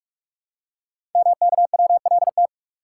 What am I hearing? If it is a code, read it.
Morse code, heard as a keyed tone, about 44 wpm, 700 Hz. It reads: MKWLT
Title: Code Groups - 5 Letter:Numbers - 45wpm